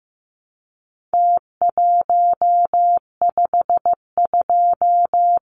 Morse code, heard as a keyed tone, about 15 wpm, 700 Hz. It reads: T152
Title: Numbers - Contest Style - 4-Digit 15wpm